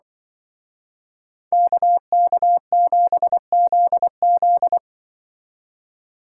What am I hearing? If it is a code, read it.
Morse code, heard as a keyed tone, about 23 wpm, 700 Hz. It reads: KK7ZZ
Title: Speed Racing - USA 2x2 Call Signs 20wpm